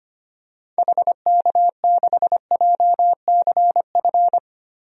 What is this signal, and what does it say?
Morse code, heard as a keyed tone, about 25 wpm, 700 Hz. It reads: HK6JCF
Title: DXCC-100 Call Signs - 6 Characters - Repeated 2x - 25wpm